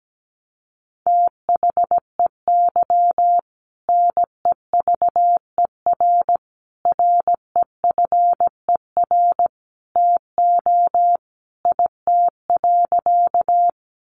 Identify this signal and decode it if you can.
Morse code, heard as a keyed tone, about 17 wpm, 700 Hz. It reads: THEY NEVER REFER TO IT.
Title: Sentences from Top 1000 Words 17wpm